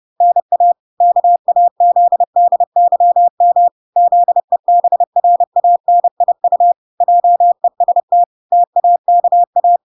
Morse code, heard as a keyed tone, about 30 wpm, 700 Hz. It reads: NA KAZDYM ZEBRANIU JEST TAKA